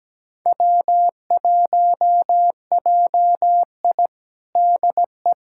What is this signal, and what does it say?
Morse code, heard as a keyed tone, about 17 wpm, 700 Hz. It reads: W1JI DE
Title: USA Callsign DE Callsign - 4 Characters - 17wpm